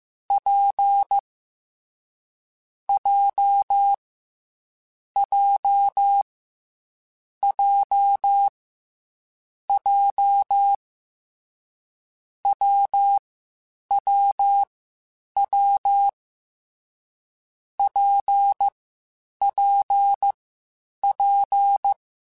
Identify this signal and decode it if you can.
Morse code, heard as a keyed tone, about 15 wpm, 800 Hz. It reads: P J J J J W W W P P P
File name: Difficult Combos / Difficult Combos-mp3s / WPJ WPJ1 @13